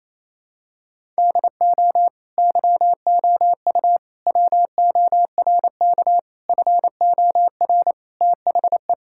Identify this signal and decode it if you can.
Morse code, heard as a keyed tone, about 28 wpm, 700 Hz. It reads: DO YOU WORK FOR THE
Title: Sentences from Top 1000 Words - Review Part 5 - 28wpm